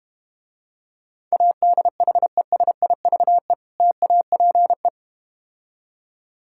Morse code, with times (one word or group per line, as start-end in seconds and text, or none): 1.3-3.5 ADHESIVE
3.8-4.9 TAPE